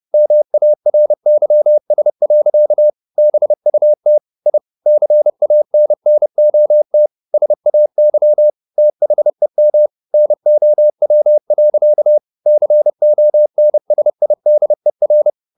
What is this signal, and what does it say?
Morse code, heard as a keyed tone, about 30 wpm, 600 Hz. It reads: MARYS. BUT I CANNOT SAY THEM NOW. CONSIDER